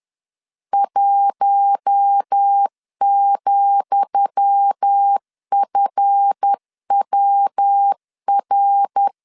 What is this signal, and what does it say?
Morse code, heard as a keyed tone, about 10 wpm, 800 Hz. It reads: 1,FWR
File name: LEZIONI / 10 SIGNS / 10D COMA 10wpm 3013